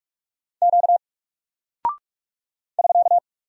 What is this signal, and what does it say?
Morse code, heard as a keyed tone, about 42 wpm, 700 Hz. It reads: Q <SK>
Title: Single Character (Letter:Number:Punctuation:Prosign) Alt 45wpm